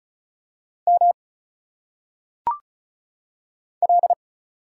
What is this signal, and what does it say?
Morse code, heard as a keyed tone, about 36 wpm, 700 Hz. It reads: M L
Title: Single Letters - Phonetic Pronunciation 35wpm